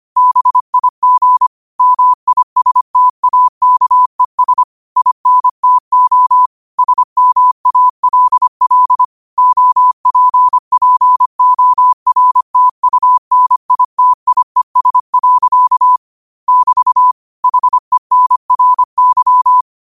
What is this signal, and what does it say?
Morse code, heard as a keyed tone, about 25 wpm, 1000 Hz. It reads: BIG MISTAKES INTO SMALL OPPORTUNITIES. <BT> HENRY